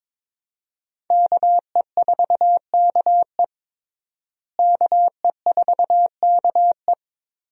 Morse code, heard as a keyed tone, about 22 wpm, 700 Hz. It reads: KE4KE KE4KE
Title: USA Call Signs - 5 Characters - Repeated 2x - 22wpm